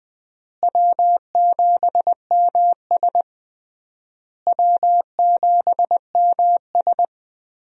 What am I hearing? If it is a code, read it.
Morse code, heard as a keyed tone, about 20 wpm, 700 Hz. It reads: W7MS W7MS
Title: USA Call Signs - 4 Characters - Repeated 3x - 20wpm